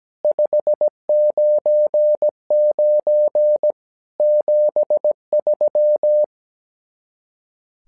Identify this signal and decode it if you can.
Morse code, heard as a keyed tone, about 17 wpm, 600 Hz. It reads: 599 73
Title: SOTA QSO Practice 17wpm